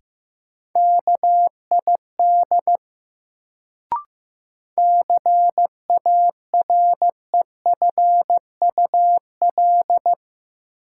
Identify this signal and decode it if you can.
Morse code, heard as a keyed tone, about 15 wpm, 700 Hz. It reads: KID CAREFUL
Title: Sentences from Top 1600 Words - New Word Drill - 15wpm